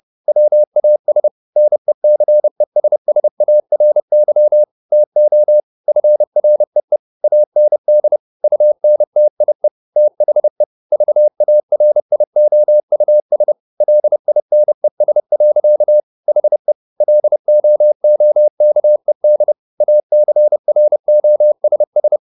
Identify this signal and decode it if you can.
Morse code, heard as a keyed tone, about 30 wpm, 600 Hz. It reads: WAS NECESSARY TO FREE AND UNTIE THE VARIOUS LINES. HE LOOKED ACROSS
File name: the old man and the sea / the old man and the sea 1 30wpm